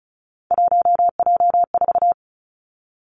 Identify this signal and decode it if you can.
Morse code, heard as a keyed tone, about 36 wpm, 700 Hz. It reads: 1J4
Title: Three Challenging Letter-Number Combinations 35wpm